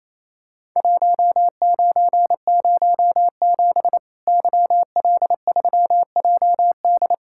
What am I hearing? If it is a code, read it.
Morse code, heard as a keyed tone, about 28 wpm, 700 Hz. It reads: 1907 YL3JD
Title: Worked All Europe DX Contest - QTC Exchange - 28wpm